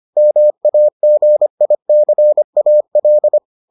Morse code, heard as a keyed tone, about 25 wpm, 600 Hz. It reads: MAGICAL